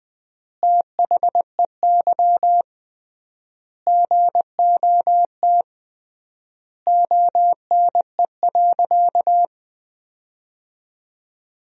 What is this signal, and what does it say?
Morse code, heard as a keyed tone, about 20 wpm, 700 Hz. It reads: THEY GOT ONE.
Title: 3x Word-Spacing - Sentences from Top 300 Words - 20wpm